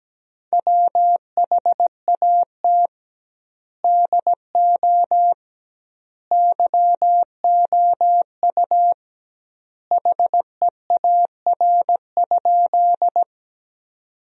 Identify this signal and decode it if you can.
Morse code, heard as a keyed tone, about 17 wpm, 700 Hz. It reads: WHAT DO YOU HEAR?
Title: 2x Word-Spacing - Sentences from Top 300 Words - 17wpm